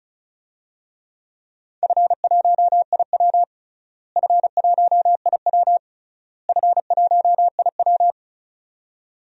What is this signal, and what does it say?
Morse code, heard as a keyed tone, about 36 wpm, 690 Hz. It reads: F1IW F1IW F1IW
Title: DXCC-100 Call Signs - 4 Characters - Repeated 3x - 35wpm